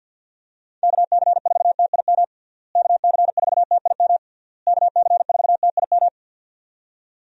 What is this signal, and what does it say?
Morse code, heard as a keyed tone, about 53 wpm, 700 Hz. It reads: KC4TIM KC4TIM KC4TIM
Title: USA Call Signs - 6 Characters - Repeated 3x - 50wpm